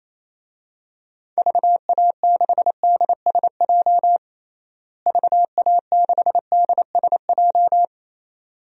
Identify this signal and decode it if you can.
Morse code, heard as a keyed tone, about 28 wpm, 700 Hz. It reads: VA6DSJ VA6DSJ
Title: DXCC-100 Call Signs - 6 Characters - Repeated 2x - 28wpm